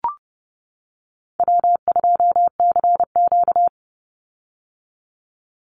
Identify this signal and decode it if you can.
Morse code, heard as a keyed tone, about 30 wpm, 700 Hz. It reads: W2CQ
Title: Call Signs - 30wpm